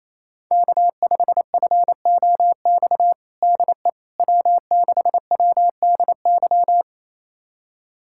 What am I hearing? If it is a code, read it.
Morse code, heard as a keyed tone, about 27 wpm, 700 Hz. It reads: K5FOX DE W6WDY
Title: USA Callsign DE Callsign - 5 Characters - 28wpm